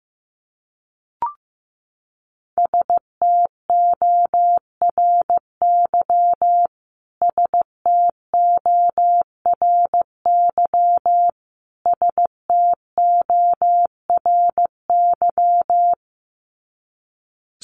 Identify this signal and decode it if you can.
Morse code, heard as a keyed tone, about 15 wpm, 700 Hz. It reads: STORY STORY STORY
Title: Top 300 Words Repeated 15wpm